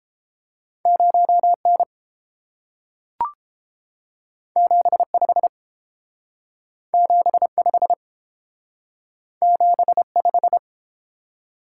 Morse code, heard as a keyed tone, about 30 wpm, 700 Hz. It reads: 0N 75 75 75